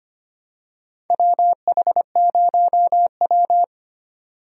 Morse code, heard as a keyed tone, about 25 wpm, 700 Hz. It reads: WH0W
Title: USA Call Signs - 4 Characters - 25wpm